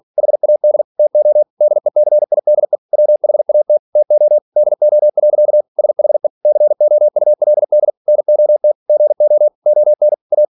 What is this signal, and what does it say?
Morse code, heard as a keyed tone, about 46 wpm, 600 Hz. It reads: HAD TO DECIDE WHAT TO DO. SHE COULD NOT GO ON A
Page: Never 47wpm